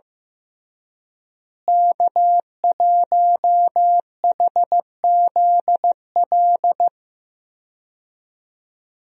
Morse code, heard as a keyed tone, about 15 wpm, 700 Hz. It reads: K1HZL
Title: USA Call Signs - 5 Characters - Repeated 3x - 15wpm